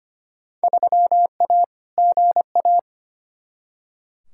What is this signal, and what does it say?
Morse code, heard as a keyed tone, about 25 wpm, 700 Hz. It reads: 3A GA
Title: Field Day Exchanges - 25wpm